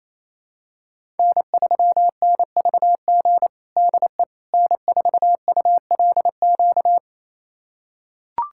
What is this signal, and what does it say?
Morse code, heard as a keyed tone, about 28 wpm, 700 Hz. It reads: N3NVG DE N4ULQ
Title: USA Callsign DE Callsign - 5 Characters - 28wpm